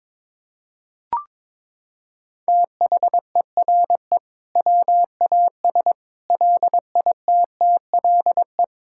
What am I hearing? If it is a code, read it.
Morse code, heard as a keyed tone, about 22 wpm, 700 Hz. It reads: THERE WAS LITTLE
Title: Sentences from Top 1000 Words - Review Part 5 - 22wpm